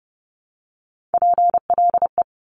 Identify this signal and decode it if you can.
Morse code, heard as a keyed tone, about 30 wpm, 690 Hz. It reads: PLE
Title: Top Three Letter Combinations in Words 30wpm